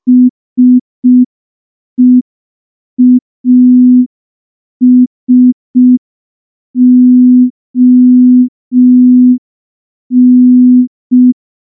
Morse code, heard as a keyed tone, about 5 wpm, 250 Hz. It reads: SEASON
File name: morse / morse-training-05WPM